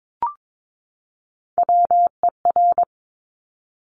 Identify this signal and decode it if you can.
Morse code, heard as a keyed tone, about 22 wpm, 700 Hz. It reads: WER
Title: Top Three Letter Combinations in Words 22wpm